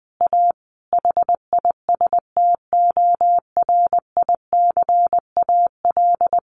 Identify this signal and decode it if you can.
Morse code, heard as a keyed tone, about 20 wpm, 700 Hz. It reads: A HISTORICAL
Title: Sentences from Top 600 Words 20wpm